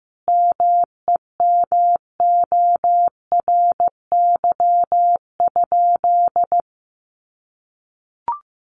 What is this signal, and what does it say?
Morse code, heard as a keyed tone, about 15 wpm, 700 Hz. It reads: MEMORY?
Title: Sentences from Top 1300 Words 15wpm